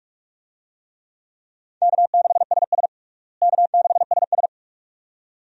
Morse code, heard as a keyed tone, about 43 wpm, 700 Hz. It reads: K6SS K6SS
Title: USA Call Signs - 4 Characters - Repeated 2x - 45wpm